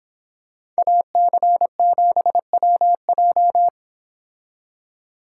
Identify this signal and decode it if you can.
Morse code, heard as a keyed tone, about 26 wpm, 700 Hz. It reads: AC7WJ
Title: Speed Racing - USA 2x2 Call Signs 22wpm